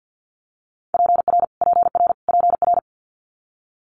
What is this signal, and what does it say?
Morse code, heard as a keyed tone, about 50 wpm, 700 Hz. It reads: PR PR PR
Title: Multi-letter Phonemes Repeated 3x 50wpm